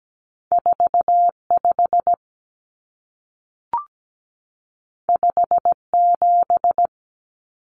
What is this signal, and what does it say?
Morse code, heard as a keyed tone, about 17 wpm, 700 Hz. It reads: 45 57
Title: Numbers - 2-Digit 17wpm